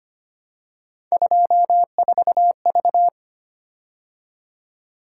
Morse code, heard as a keyed tone, about 30 wpm, 700 Hz. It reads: 24V
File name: Speed-Racing - Three Letter-Number Combinations - 25wpm